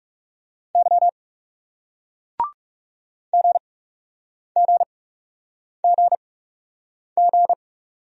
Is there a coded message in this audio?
Yes: Morse code from 0.8 to 7.5 s, heard as a keyed tone, about 41 wpm, spaced as for 8 wpm, 700 Hz. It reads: Y GGGG